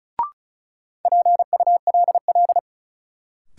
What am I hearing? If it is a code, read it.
Morse code, heard as a keyed tone, about 36 wpm, 700 Hz. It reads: PULL
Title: Sentences from Top 400 Words - New Word Drill - 35wpm